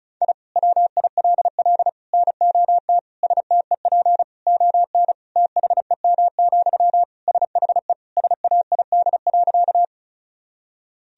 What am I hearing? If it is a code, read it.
Morse code, heard as a keyed tone, about 36 wpm, 700 Hz. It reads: I WILL NOT STEP ON THEM, SHE SAID.
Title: Sentences from Top 1000 Words - Review Part 2 - 35wpm